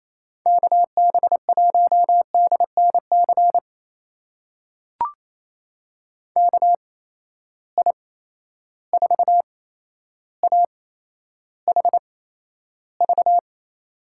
Morse code, heard as a keyed tone, about 28 wpm, spaced as for 12 wpm, 700 Hz. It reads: KB1DNC KI4AHV
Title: USA Call Signs - 6 Characters - 8x Character Spacing - 28wpm